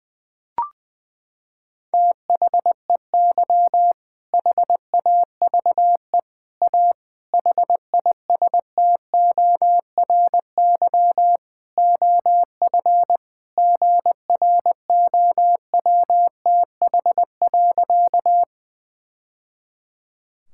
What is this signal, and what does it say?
Morse code, heard as a keyed tone, about 21 wpm, 700 Hz. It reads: THEY HAVE A HISTORY OF GROWTH.
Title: Sentences from Top 1000 Words - Review Part 5 - 20wpm